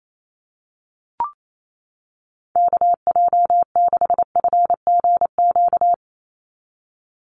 Morse code, heard as a keyed tone, about 28 wpm, 700 Hz. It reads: KJ6FGQ
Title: USA Call Signs 28wpm